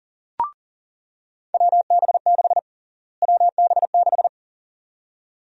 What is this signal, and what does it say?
Morse code, heard as a keyed tone, about 40 wpm, 700 Hz. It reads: WB6 WB6
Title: USA Call Sign Prefixes - Repeated 2x - 40wpm